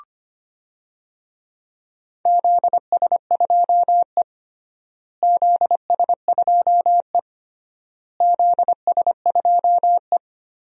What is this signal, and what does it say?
Morse code, heard as a keyed tone, about 25 wpm, 700 Hz. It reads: ZS2E ZS2E ZS2E
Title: DXCC-100 Call Signs - 4 Characters - Repeated 3x - 25wpm